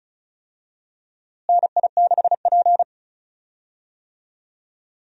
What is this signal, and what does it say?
Morse code, heard as a keyed tone, about 34 wpm, 700 Hz. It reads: NI6P
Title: USA Call Signs - 4 Characters - Repeated 2x - 35wpm